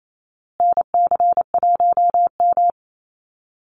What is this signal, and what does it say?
Morse code, heard as a keyed tone, about 28 wpm, 700 Hz. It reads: NC1M
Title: Call Signs Repeated 2x - 28wpm